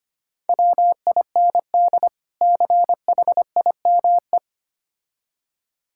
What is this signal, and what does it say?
Morse code, heard as a keyed tone, about 25 wpm, 700 Hz. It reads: WIND CHIME
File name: practice / 25wpm / Difficult Words with Hints - 25wpm